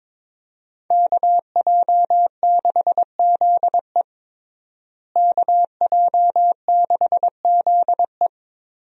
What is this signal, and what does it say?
Morse code, heard as a keyed tone, about 22 wpm, 700 Hz. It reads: KJ6ZE KJ6ZE
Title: USA Call Signs - 5 Characters - Repeated 3x - 22wpm